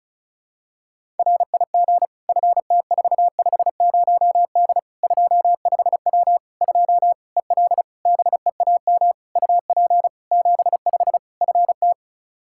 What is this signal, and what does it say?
Morse code, heard as a keyed tone, about 36 wpm, 700 Hz. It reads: RIG FT450D 25W 2 EL BEAM UP 75 FT
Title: Practice QSO - Exchange 2 - Info - 35wpm